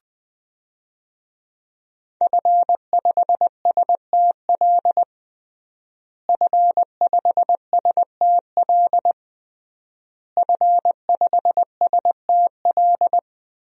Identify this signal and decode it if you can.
Morse code, heard as a keyed tone, about 20 wpm, 700 Hz. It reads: F5STL F5STL F5STL
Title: DXCC-100 Call Signs - 5 Characters - Repeated 3x - 20wpm